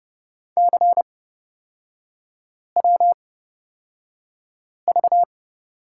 Morse code, heard as a keyed tone, about 30 wpm, spaced as for 5 wpm, 690 Hz. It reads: CWV
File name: Lesson 37e - Letter x - Three Characters 30wpm